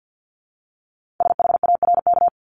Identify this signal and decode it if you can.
Morse code, heard as a keyed tone, about 50 wpm, 700 Hz. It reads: SHARK
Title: Multi-letter Phonemes 50wpm